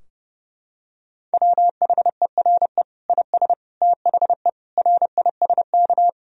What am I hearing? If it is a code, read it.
Morse code, heard as a keyed tone, about 30 wpm, 700 Hz. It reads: WHERE IS THE RISK
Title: Sentences from Top 1000 Words - Review Part 4 - 30wpm